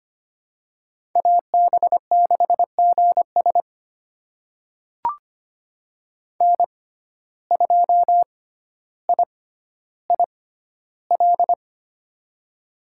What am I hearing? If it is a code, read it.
Morse code, heard as a keyed tone, about 25 wpm, spaced as for 13 wpm, 700 Hz. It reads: AB6GS N2IIL
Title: USA Call Signs - 5 Characters - 6x Character Spacing - 25wpm